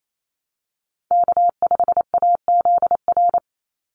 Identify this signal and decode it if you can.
Morse code, heard as a keyed tone, about 28 wpm, 700 Hz. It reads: K5AZR
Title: Call Signs - 28wpm